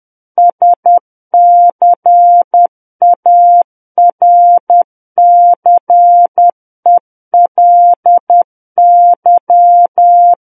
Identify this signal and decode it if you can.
Morse code, heard as a keyed tone, about 10 wpm, 700 Hz. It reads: SCARCELY
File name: Book One / 01-Chapter 1 - 10 WPM